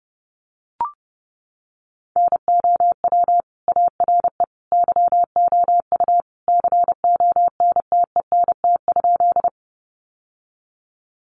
Morse code, heard as a keyed tone, about 30 wpm, 700 Hz. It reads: NOW ARE YOU CONTENT?
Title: Sentences from Top 900 Words 30wpm